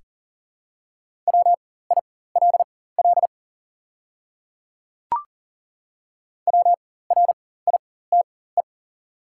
Morse code, heard as a keyed tone, about 40 wpm, spaced as for 19 wpm, 700 Hz. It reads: WILL WRITE
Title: 4x Character Spacing - Top 100 Words - 40wpm